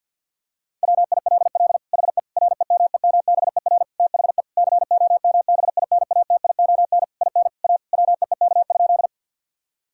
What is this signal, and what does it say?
Morse code, heard as a keyed tone, about 49 wpm, 700 Hz. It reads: WILL HE REMEMBER THE COMBINATION IN A WEEK?
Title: Sentences from Top 1000 Words - Review Part 6 - 50wpm